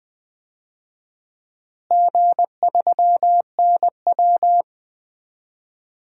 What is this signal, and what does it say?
Morse code, heard as a keyed tone, about 21 wpm, 700 Hz. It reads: G3NW
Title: DXCC-100 Call Signs - 4 Characters - Repeated 3x - 20wpm